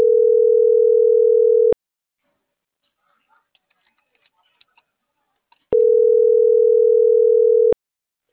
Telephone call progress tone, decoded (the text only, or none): ringback tone